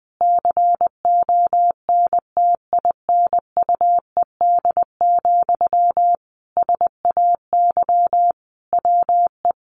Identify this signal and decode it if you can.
Morse code, heard as a keyed tone, about 19 wpm, 700 Hz. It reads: CONTINUED, SAY WE
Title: A Sound Of Thunder 20wpm